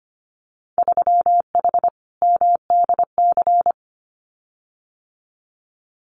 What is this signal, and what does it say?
Morse code, heard as a keyed tone, about 25 wpm, 700 Hz. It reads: 3H MDC
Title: Winter Field Day Exchanges Encore 25wpm